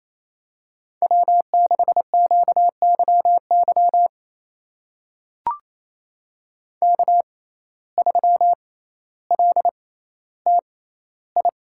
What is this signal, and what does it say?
Morse code, heard as a keyed tone, about 28 wpm, spaced as for 14 wpm, 700 Hz. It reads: W6QYY K3LTI
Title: USA Call Signs - 5 Characters - 6x Character Spacing - 28wpm